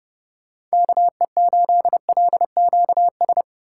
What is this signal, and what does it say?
Morse code, heard as a keyed tone, about 30 wpm, 700 Hz. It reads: KE8LQS